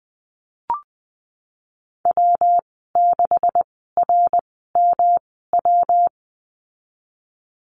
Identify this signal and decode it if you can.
Morse code, heard as a keyed tone, about 20 wpm, spaced as for 15 wpm, 700 Hz. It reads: W6RMW
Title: 173 Lesson 28g - Number 9 - Call Signs 20wpm